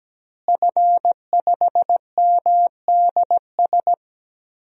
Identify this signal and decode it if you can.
Morse code, heard as a keyed tone, about 17 wpm, 700 Hz. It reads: F5MDS
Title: DXCC-100 Call Signs - 5 Characters - Repeated 3x - 17wpm